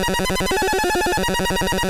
There is no code.